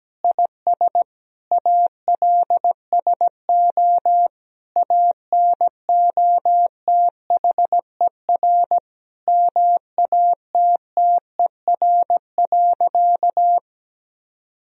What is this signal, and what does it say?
Morse code, heard as a keyed tone, about 17 wpm, 700 Hz. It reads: IS ALSO ANOTHER MATTER.
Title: Sentences from Top 1000 Words - Review Part 1 - 17wpm